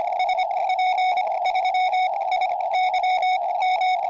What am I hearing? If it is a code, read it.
Morse code, heard as a keyed tone, about 25 wpm, spaced as for 19 wpm, 2200 Hz, with radio noise. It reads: SP3IYM
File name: tropo / yl3iq